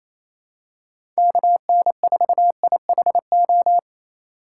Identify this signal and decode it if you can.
Morse code, heard as a keyed tone, about 28 wpm, 700 Hz. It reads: KN4IHO
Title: USA Call Signs - 6 Characters - 4x Character Spacing - 28wpm